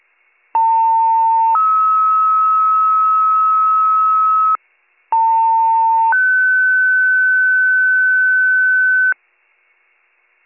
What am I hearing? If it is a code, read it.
Two-tone page, heard as a one-second tone, then a three-second tone, about 900 then 1290 Hz, about 900 then 1500 Hz. It is heard from 0.5 to 9.1 s.